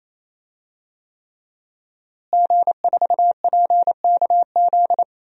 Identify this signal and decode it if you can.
Morse code, heard as a keyed tone, about 28 wpm, 700 Hz. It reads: G4PKZ
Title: DXCC-100 Call Signs - 5 Characters - Repeated 3x - 28wpm